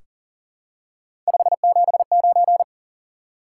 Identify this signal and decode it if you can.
Morse code, heard as a keyed tone, about 38 wpm, 700 Hz. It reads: 579
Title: Top 180 QSO Elements 40wpm